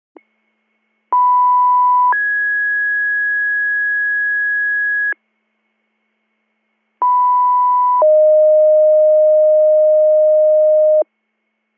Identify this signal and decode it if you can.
Two-tone page, heard as a one-second tone, then a three-second tone, about 990 then 1650 Hz, about 990 then 620 Hz. It is heard from 1.1 to 11.0 s.